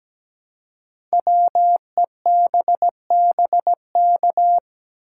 Morse code, heard as a keyed tone, about 17 wpm, 700 Hz. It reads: WEBBK
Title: Code Groups - 5 Letter:Numbers - 17wpm